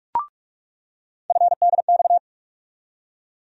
Morse code, heard as a keyed tone, about 43 wpm, 700 Hz. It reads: FDX
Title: USA Call Sign Postfixes - 3 Letters 45wpm